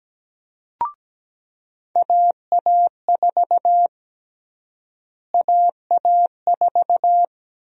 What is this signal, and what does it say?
Morse code, heard as a keyed tone, about 17 wpm, 700 Hz. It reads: AA4 AA4
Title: USA Call Sign Prefixes - Repeated 2x - 17wpm